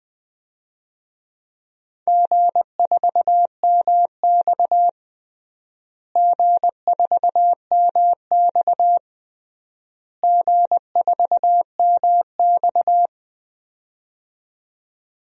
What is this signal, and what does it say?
Morse code, heard as a keyed tone, about 20 wpm, 700 Hz. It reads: G4MX G4MX G4MX
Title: DXCC-100 Call Signs - 4 Characters - Repeated 3x - 20wpm